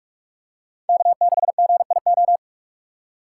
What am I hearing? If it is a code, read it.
Morse code, heard as a keyed tone, about 43 wpm, 700 Hz. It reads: K6GIO
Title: USA Call Signs - 5 Characters - 2x Character Spacing - 45wpm